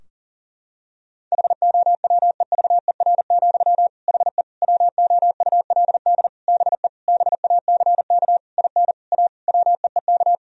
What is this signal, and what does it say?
Morse code, heard as a keyed tone, about 40 wpm, 700 Hz. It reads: HOWEVER, HE WOULD BE BACK IN A WEEK
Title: Sentences from Top 1000 Words - Review Part 2 - 40wpm